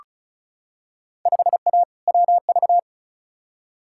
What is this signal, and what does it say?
Morse code, heard as a keyed tone, about 36 wpm, 700 Hz. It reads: 5A WV
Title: Field Day Exchanges - 35wpm